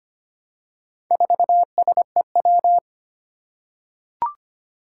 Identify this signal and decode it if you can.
Morse code, heard as a keyed tone, about 25 wpm, 700 Hz. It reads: 4SEW
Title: Four Challenging Letter-Number Combinations 25wpm